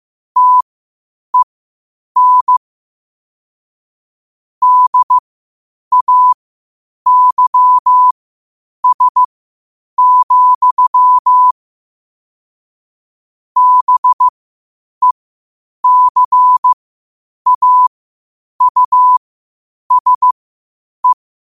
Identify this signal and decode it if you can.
Morse code, heard as a keyed tone, about 15 wpm, spaced as for 8 wpm, 1000 Hz. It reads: TEN DAYS, BECAUSE
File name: qotdfiles / 8wpm